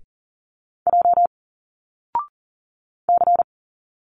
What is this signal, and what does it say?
Morse code, heard as a keyed tone, about 40 wpm, 700 Hz. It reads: J C
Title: Single Letters 40wpm